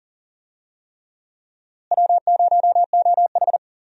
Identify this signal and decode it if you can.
Morse code, heard as a keyed tone, about 40 wpm, 700 Hz. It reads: W0OH
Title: USA Call Signs - 4 Characters - Repeated 3x - 40wpm